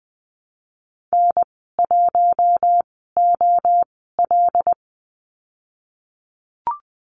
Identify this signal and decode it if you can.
Morse code, heard as a keyed tone, about 20 wpm, spaced as for 15 wpm, 700 Hz. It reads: N1OL
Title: 122 Lesson 20g - Number 3 - Call Signs 20wpm